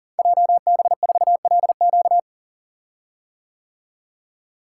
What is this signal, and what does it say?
Morse code, heard as a keyed tone, about 38 wpm, 700 Hz. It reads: JB4LQ
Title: Code Groups - 5 Letter:Numbers - 40wpm